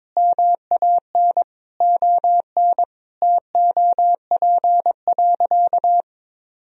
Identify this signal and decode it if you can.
Morse code, heard as a keyed tone, about 22 wpm, 700 Hz. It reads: MAN ON TOP.
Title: Sentences from Top 1000 Words - Review Part 2 - 22wpm